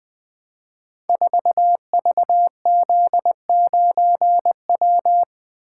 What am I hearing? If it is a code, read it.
Morse code, heard as a keyed tone, about 20 wpm, 700 Hz. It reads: 4VZ9W
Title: Code Groups - 5 Letter:Numbers - 20wpm